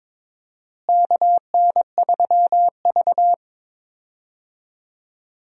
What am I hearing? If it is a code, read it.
Morse code, heard as a keyed tone, about 22 wpm, 700 Hz. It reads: KN3V
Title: USA Call Signs - 4 Characters - Repeated 3x - 22wpm